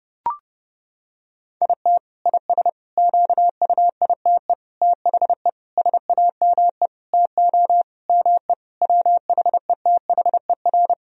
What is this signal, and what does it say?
Morse code, heard as a keyed tone, about 30 wpm, 700 Hz. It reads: IT IS QUITE THE SAME TO ME WHETHER
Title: Sentences from Top 1000 Words - Review Part 3 - 30wpm